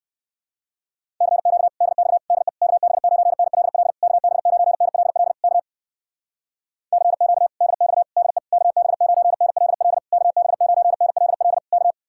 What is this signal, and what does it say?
Morse code, heard as a keyed tone, about 61 wpm, 700 Hz. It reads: CQ DX DE KB9NLL KB9NLL K CQ DX DE KB9NLL KB9NLL K